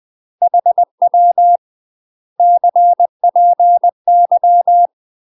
Morse code, heard as a keyed tone, about 20 wpm, 700 Hz. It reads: HW CPY
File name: UK MASTER QSOs TRAINING/20WPM